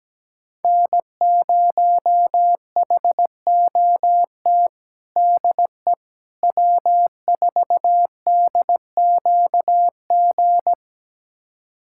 Morse code, heard as a keyed tone, about 17 wpm, 700 Hz. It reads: N0HOT DE W4DQG
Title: USA Callsign DE Callsign - 5 Characters - 17wpm